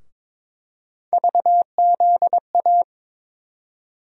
Morse code, heard as a keyed tone, about 22 wpm, 700 Hz. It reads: VZA
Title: USA Call Sign Postfixes - 3 Letters 22wpm